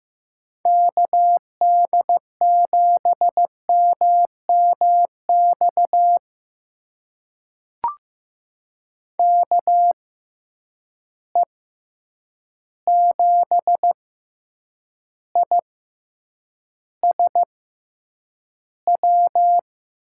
Morse code, heard as a keyed tone, about 15 wpm, spaced as for 8 wpm, 700 Hz. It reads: KD7MMX KE7ISW
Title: USA Call Signs - 6 Characters - 6x Character Spacing - 15wpm